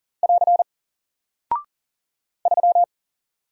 Morse code, heard as a keyed tone, about 40 wpm, 700 Hz. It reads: <AR> 3
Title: Single Character (Letter:Number:Punctuation:Prosign) Alt 40wpm